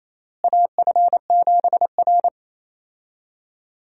Morse code, heard as a keyed tone, about 28 wpm, 700 Hz. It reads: AF7R
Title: USA Call Signs - 4 Characters - Repeated 2x - 28wpm